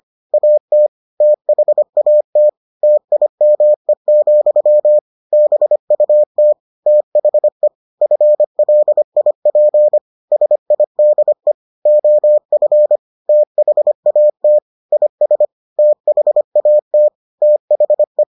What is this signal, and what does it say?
Morse code, heard as a keyed tone, about 25 wpm, 600 Hz. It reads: AT THAT TIME, BUT THE FLIP SIDE OF THAT IS THAT THE